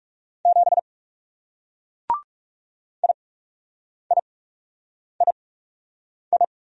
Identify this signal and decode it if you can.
Morse code, heard as a keyed tone, about 38 wpm, spaced as for 8 wpm, 700 Hz. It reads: 7 IIII